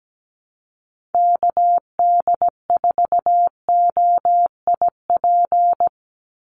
Morse code, heard as a keyed tone, about 17 wpm, 700 Hz. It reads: KD4OIP
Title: USA Call Signs 17wpm